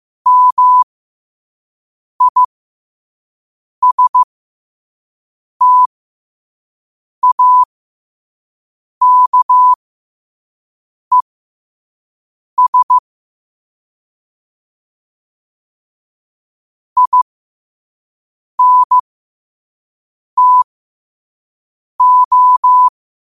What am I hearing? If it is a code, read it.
Morse code, heard as a keyed tone, about 15 wpm, spaced as for 5 wpm, 1000 Hz. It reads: MISTAKES INTO
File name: qotdfiles / 5wpm